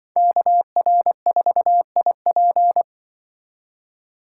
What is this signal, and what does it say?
Morse code, heard as a keyed tone, about 24 wpm, 700 Hz. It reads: KR4IP
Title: Speed Racing - USA 2x2 Call Signs 20wpm